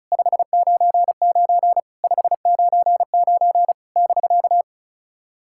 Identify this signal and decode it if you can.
Morse code, heard as a keyed tone, about 36 wpm, 690 Hz. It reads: 599 599 <BK>
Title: SOTA QSO Practice 35wpm